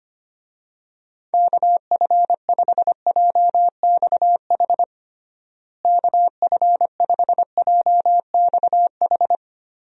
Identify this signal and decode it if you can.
Morse code, heard as a keyed tone, about 25 wpm, 700 Hz. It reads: KF5JXH KF5JXH
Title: USA Call Signs - 6 Characters - Repeated 2x - 25wpm